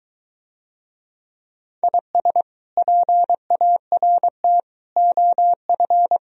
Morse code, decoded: IS PART OF

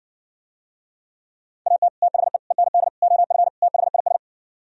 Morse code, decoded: AT THE END OF THIS

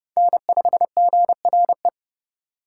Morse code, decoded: N5GRE

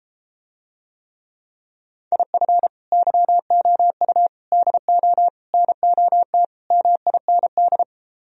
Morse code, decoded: IF YOU DO NOT MIND

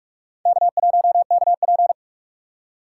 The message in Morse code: K1KP

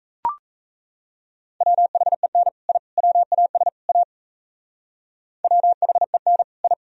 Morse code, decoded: WHEN I WAS A WHEN I

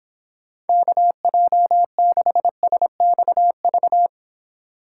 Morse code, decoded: KJ6SXV

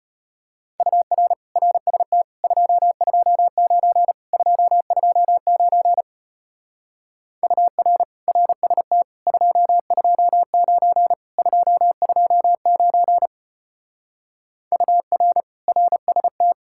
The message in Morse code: UR RST 229 229 UR RST 229 229 UR RST

